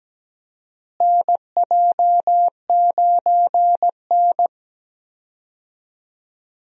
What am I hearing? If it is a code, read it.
Morse code, heard as a keyed tone, about 17 wpm, 700 Hz. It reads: NJ9N
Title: USA Call Signs - 4 Characters - Repeated 2x - 17wpm